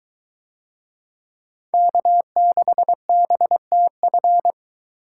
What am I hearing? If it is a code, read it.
Morse code, heard as a keyed tone, about 22 wpm, 700 Hz. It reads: K6BTF